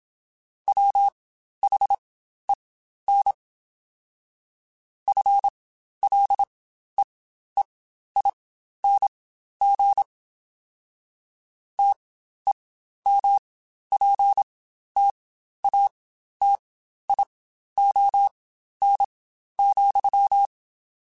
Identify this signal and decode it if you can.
Morse code, heard as a keyed tone, about 27 wpm, spaced as for 12 wpm, 800 Hz. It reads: WHEN FLEEING TEMPTATION,
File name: Spaced / Chuckles Spaced-mp3s / Temptation sp @25